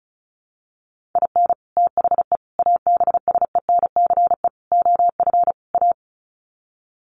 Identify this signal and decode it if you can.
Morse code, heard as a keyed tone, about 34 wpm, 700 Hz. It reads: IN THE ABSENCE OF A
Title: Sets of 5 Words Encore 35wpm